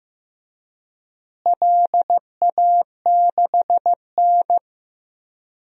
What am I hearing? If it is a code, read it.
Morse code, heard as a keyed tone, about 15 wpm, 700 Hz. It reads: LA6N
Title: DXCC-100 Call Signs - 4 Characters - Repeated 3x - 15wpm